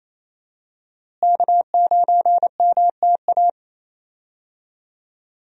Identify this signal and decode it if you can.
Morse code, heard as a keyed tone, about 28 wpm, 700 Hz. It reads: K9MTA
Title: USA Call Signs - 5 Characters - Repeated 2x - 28wpm